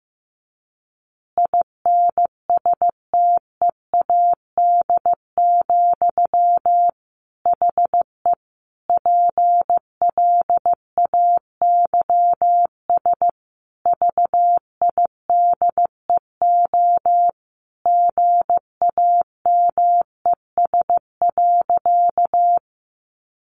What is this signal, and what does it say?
Morse code, heard as a keyed tone, about 15 wpm, 700 Hz. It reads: INSTEAD, HE PLAYS VIDEO GAMES.
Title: Additional Easy Phrases 15wpm